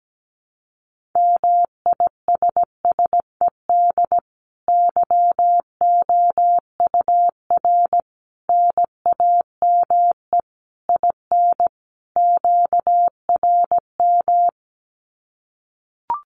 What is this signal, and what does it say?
Morse code, heard as a keyed tone, about 17 wpm, 700 Hz. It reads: MISSED YOUR NAME IN QRM
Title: Common QSO Phrases 17wpm